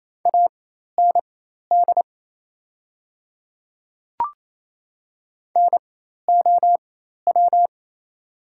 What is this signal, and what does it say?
Morse code, heard as a keyed tone, about 27 wpm, spaced as for 13 wpm, 690 Hz. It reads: AND NOW
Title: 4x Character Spacing - Top 100 Words - 28wpm